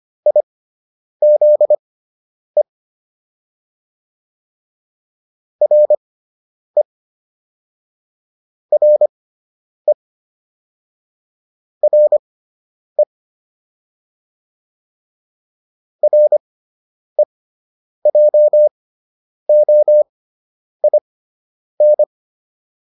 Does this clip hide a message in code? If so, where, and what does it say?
Yes: Morse code from 0.3 to 22.0 s, heard as a keyed tone, about 25 wpm, spaced as for 9 wpm, 600 Hz. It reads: IZE RE RE RE REJOIN